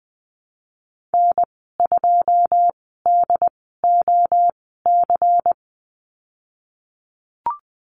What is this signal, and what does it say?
Morse code, heard as a keyed tone, about 19 wpm, spaced as for 14 wpm, 700 Hz. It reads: N2DOC
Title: 128 Lesson 21g - Number 6 - Call Signs 20wpm